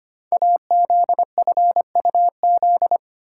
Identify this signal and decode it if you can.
Morse code, heard as a keyed tone, about 25 wpm, 700 Hz. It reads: AZFUZ
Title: Code Groups - 5 Letters - 25wpm